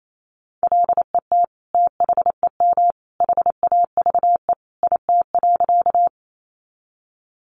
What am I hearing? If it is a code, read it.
Morse code, heard as a keyed tone, about 28 wpm, 700 Hz. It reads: LET THEM HAVE IT.
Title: Sentences from Top 300 Words 28wpm